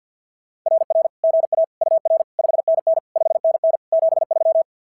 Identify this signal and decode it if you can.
Morse code, heard as a keyed tone, about 52 wpm, 690 Hz. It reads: RR GA UR 5NN 5NN 73
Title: SOTA QSO Practice 50wpm